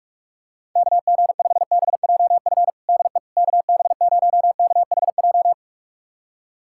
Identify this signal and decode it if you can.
Morse code, heard as a keyed tone, about 46 wpm, 700 Hz. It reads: KG5BJF DE KB0KHJ
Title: USA Callsign DE Callsign - 6 Characters - 45wpm